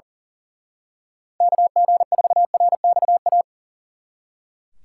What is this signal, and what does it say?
Morse code, heard as a keyed tone, about 38 wpm, 700 Hz. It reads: KG4RXA
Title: Speed Racing - USA Call Signs 40wpm